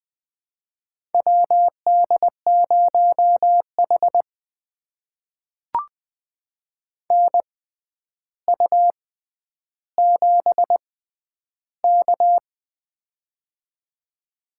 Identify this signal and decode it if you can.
Morse code, heard as a keyed tone, about 20 wpm, spaced as for 10 wpm, 700 Hz. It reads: WD0H NU7K
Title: USA Call Signs - 4 Characters - 6x Character Spacing - 20wpm